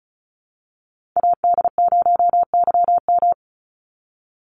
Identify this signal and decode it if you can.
Morse code, heard as a keyed tone, about 34 wpm, 700 Hz. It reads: AD0YM
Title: NA SOTA Chaser Callsigns 35wpm